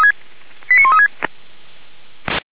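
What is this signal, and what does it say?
Five-tone selective call (ZVEI), heard as quick runs of five tones, tones of about 70 ms. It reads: 89136
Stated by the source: SDR receiver recording of ZVEI five-tone selective call 89135, from feuerwehr-orlamuende.org fire brigade server.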